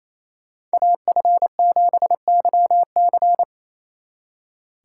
Morse code, heard as a keyed tone, about 33 wpm, 700 Hz. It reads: AF7YC
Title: Speed Racing - USA 2x2 Call Signs 28wpm